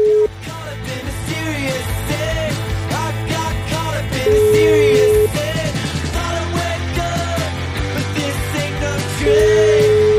Ringback tone (Europe, 425 Hz), with noise.